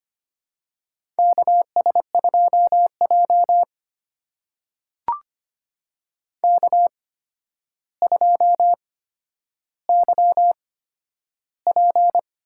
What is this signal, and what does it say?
Morse code, heard as a keyed tone, about 25 wpm, spaced as for 11 wpm, 700 Hz. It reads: KS2J K2YP
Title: USA Call Signs - 4 Characters - 8x Character Spacing - 25wpm